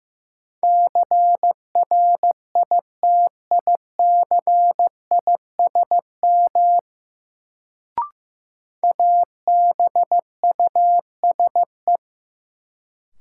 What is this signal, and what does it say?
Morse code, heard as a keyed tone, about 15 wpm, 700 Hz. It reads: CRITICISM ABUSE
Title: Sentences from Top 1300 Words - New Word Drill - 15wpm